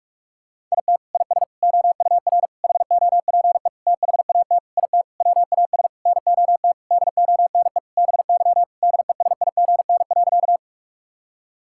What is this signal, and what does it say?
Morse code, heard as a keyed tone, about 47 wpm, 700 Hz. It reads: IT IS OUR HOPE THAT IT WAS NOT DONE BY DESIGN.
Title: Sentences from Top 1000 Words - Review Part 5 - 45wpm